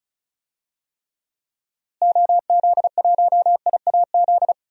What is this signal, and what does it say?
Morse code, heard as a keyed tone, about 34 wpm, 700 Hz. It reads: OZ1IAZ
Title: DXCC-100 Call Signs - 6 Characters - Repeated 2x - 35wpm